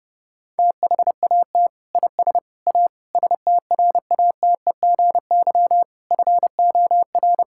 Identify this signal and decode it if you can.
Morse code, heard as a keyed tone, about 30 wpm, 700 Hz. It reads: THAT IS A STRATEGY FOR
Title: Sentences from Top 1000 Words - Review Part 5 - 30wpm